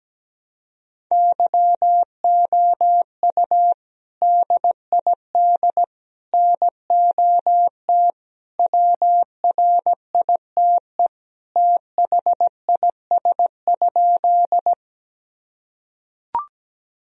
Morse code, heard as a keyed tone, about 17 wpm, 700 Hz. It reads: YOU DID NOT WRITE THIS?
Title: Sentences from Top 1000 Words - Review Part 1 - 17wpm